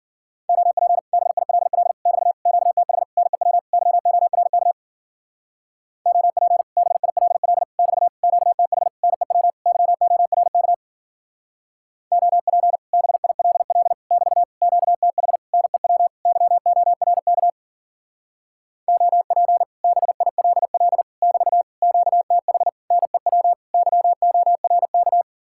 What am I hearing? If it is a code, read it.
Morse code, heard as a keyed tone, about 51 wpm, 700 Hz. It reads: OP BILL <BT> QTH NEW YORK OP BILL <BT> QTH NEW YORK OP BILL <BT> QTH NEW YORK OP BILL <BT> QTH NEW YORK